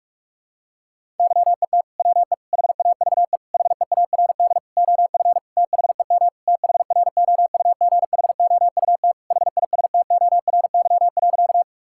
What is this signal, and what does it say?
Morse code, heard as a keyed tone, about 46 wpm, 690 Hz. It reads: YET WE HAVE HEARD OF THEM THROUGHOUT HISTORY.
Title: Sentences from Top 1000 Words - Review Part 5 - 45wpm